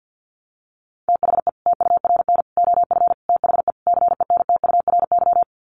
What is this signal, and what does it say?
Morse code, heard as a keyed tone, about 48 wpm, 700 Hz. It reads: THE TURN OF THE CENTURY
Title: Sets of 5 Words 50wpm